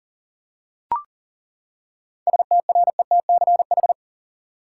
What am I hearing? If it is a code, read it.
Morse code, heard as a keyed tone, about 40 wpm, 700 Hz. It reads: STRETCH
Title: Seven Letter Words - 40wpm